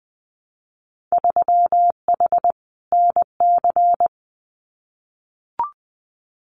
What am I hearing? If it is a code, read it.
Morse code, heard as a keyed tone, about 20 wpm, 700 Hz. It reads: 3H NC
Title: Winter Field Day Exchanges 20wpm